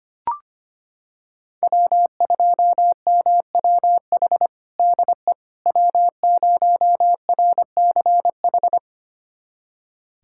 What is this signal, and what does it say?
Morse code, heard as a keyed tone, about 25 wpm, 700 Hz. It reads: W2MWH DE W0RCH
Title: USA Callsign DE Callsign - 5 Characters - 25wpm